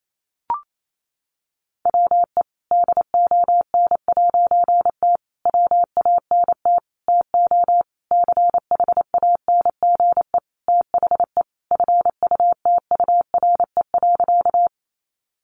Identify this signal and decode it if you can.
Morse code, heard as a keyed tone, about 28 wpm, 700 Hz. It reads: WE DON'T WANT TO CHANGE THE FUTURE.
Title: A Sound Of Thunder 28wpm